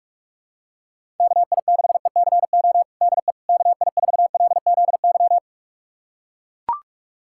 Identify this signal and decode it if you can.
Morse code, heard as a keyed tone, about 42 wpm, 700 Hz. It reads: KI6ECO DE KI4LZY
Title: USA Callsign DE Callsign - 6 Characters - 45wpm